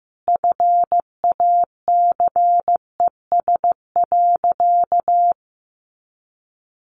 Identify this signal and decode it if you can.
Morse code, heard as a keyed tone, about 15 wpm, 700 Hz. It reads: FACES.
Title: A Sound Of Thunder 15wpm